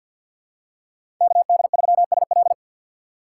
Morse code, heard as a keyed tone, about 53 wpm, 700 Hz. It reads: KD3SL
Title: USA Call Signs - 5 Characters - 3x Character Spacing - 50wpm